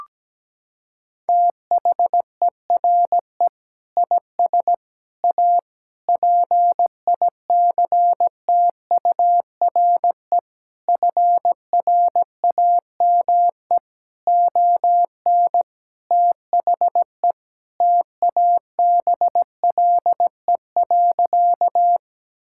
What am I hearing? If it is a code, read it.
Morse code, heard as a keyed tone, about 17 wpm, 700 Hz. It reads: THERE IS A PICTURE FRAME ON THE TABLE.
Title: Sentences from Top 1000 Words - Review Part 6 - 17wpm